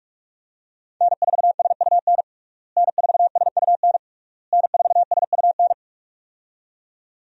Morse code, heard as a keyed tone, about 46 wpm, 700 Hz. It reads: N4SUN N4SUN N4SUN
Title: USA Call Signs - 5 Characters - Repeated 3x - 45wpm